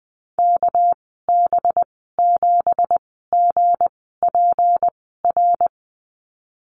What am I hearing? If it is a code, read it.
Morse code, heard as a keyed tone, about 19 wpm, spaced as for 14 wpm, 700 Hz. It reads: KB7GPR
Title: 207 Lesson 33g - Letter j - Call Signs 20wpm